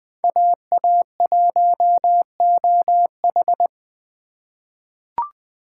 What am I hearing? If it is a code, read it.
Morse code, heard as a keyed tone, about 20 wpm, 700 Hz. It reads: AA1OH
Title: USA Call Signs - 5 Characters - 2x Character Spacing - 20wpm